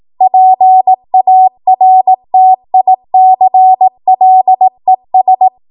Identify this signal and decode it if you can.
Morse code, heard as a keyed tone, about 18 wpm, 750 Hz. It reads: PARTICLES
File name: Morse / 18WPM